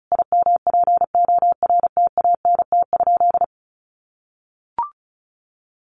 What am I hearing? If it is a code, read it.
Morse code, heard as a keyed tone, about 34 wpm, 700 Hz. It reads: IMPORTANT?
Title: Sentences from Top 1300 Words 35wpm